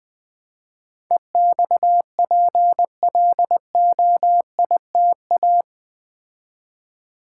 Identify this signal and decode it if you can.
Morse code, heard as a keyed tone, about 20 wpm, 700 Hz. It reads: EXPLOITA
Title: Build A Word - 4 Letters At A Time - 12-Letter Words - 20wpm